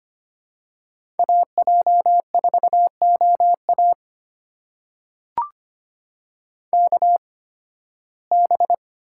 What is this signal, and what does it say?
Morse code, heard as a keyed tone, about 25 wpm, spaced as for 16 wpm, 700 Hz. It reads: AJ4OA KB
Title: USA Call Signs - 8x Character Spacing - 25wpm